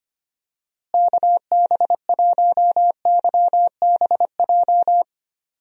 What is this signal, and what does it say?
Morse code, heard as a keyed tone, about 25 wpm, 700 Hz. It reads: KB1YBJ
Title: USA Call Signs - 6 Characters - Repeated 2x - 25wpm